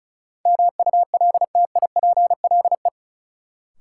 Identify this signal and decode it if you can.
Morse code, heard as a keyed tone, about 36 wpm, 700 Hz. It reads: MULTIPLE
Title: Sets of 5 Words - New Word Drill - 35wpm